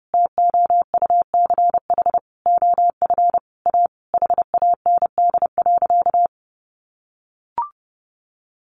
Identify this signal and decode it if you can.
Morse code, heard as a keyed tone, about 30 wpm, 700 Hz. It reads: TOUCH OF A HAND.
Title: A Sound Of Thunder 30wpm